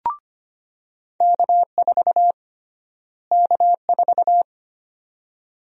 Morse code, heard as a keyed tone, about 25 wpm, 700 Hz. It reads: K4 K4
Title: USA Call Sign Prefixes - Repeated 2x - 25wpm